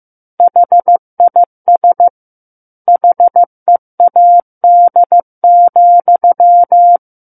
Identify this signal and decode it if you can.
Morse code, heard as a keyed tone, about 15 wpm, 700 Hz. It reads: HIS HEAD,
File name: Chapter 6 - 15 WPM